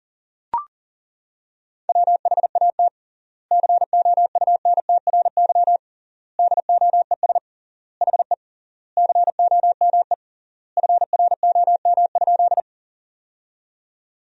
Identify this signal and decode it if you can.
Morse code, heard as a keyed tone, about 40 wpm, 700 Hz. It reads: WHAT COUNTRY DOES HE COME FROM?
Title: 3x Word-Spacing - Sentences from Top 300 Words - 40wpm